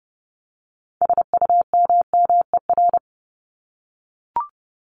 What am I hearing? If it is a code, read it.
Morse code, heard as a keyed tone, about 30 wpm, 700 Hz. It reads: SUMMER
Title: Top 1001-1250 Words 30wpm